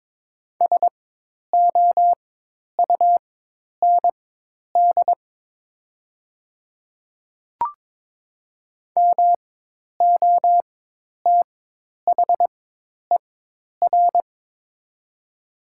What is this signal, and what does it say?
Morse code, heard as a keyed tone, about 22 wpm, spaced as for 10 wpm, 700 Hz. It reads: SOUND MOTHER
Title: 4x Character Spacing - Top 200 Words - 22wpm